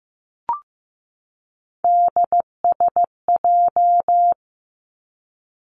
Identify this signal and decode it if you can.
Morse code, heard as a keyed tone, about 15 wpm, 700 Hz. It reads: DSJ
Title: Three Challenging Letter-Number Combinations 15wpm